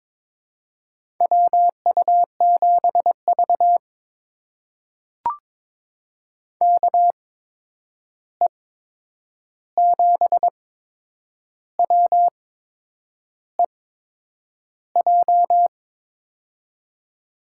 Morse code, heard as a keyed tone, about 22 wpm, spaced as for 8 wpm, 700 Hz. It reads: WU7V KE7WEJ
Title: USA Call Signs - 8x Character Spacing - 22wpm